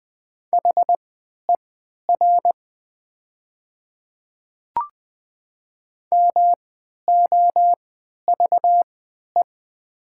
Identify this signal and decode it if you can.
Morse code, heard as a keyed tone, about 20 wpm, spaced as for 11 wpm, 690 Hz. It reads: HER MOVE
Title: 3x Character Spacing - Top 200 Words - 20wpm